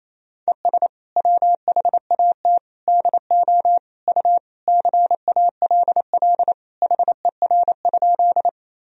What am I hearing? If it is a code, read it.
Morse code, heard as a keyed tone, about 27 wpm, 700 Hz. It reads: ES WHAT DO U CALL HER?
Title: Sentences from Top 100 Words and Abbreviations 28wpm